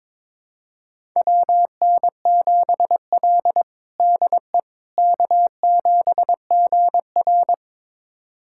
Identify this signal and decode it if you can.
Morse code, heard as a keyed tone, about 22 wpm, 700 Hz. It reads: WN7L DE K7GR
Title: USA Callsign DE Callsign - 4 Characters - 22wpm